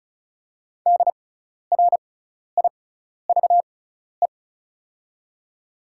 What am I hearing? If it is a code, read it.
Morse code, heard as a keyed tone, about 36 wpm, spaced as for 12 wpm, 700 Hz. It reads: DRIVE
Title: 6x Character Spacing - Top 500 Words - 35wpm